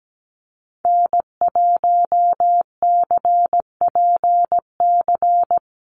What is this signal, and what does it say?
Morse code, heard as a keyed tone, about 17 wpm, 700 Hz. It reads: N1CPC
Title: USA Call Signs 17wpm